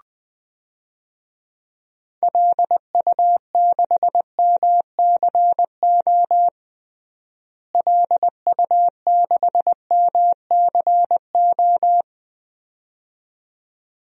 Morse code, heard as a keyed tone, about 20 wpm, 700 Hz. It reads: LU6MCO LU6MCO
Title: DXCC-100 Call Signs - 6 Characters - Repeated 2x - 20wpm